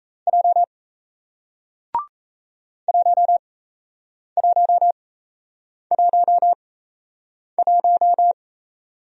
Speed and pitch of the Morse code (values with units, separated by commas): 38 wpm, spaced as for 8 wpm, 700 Hz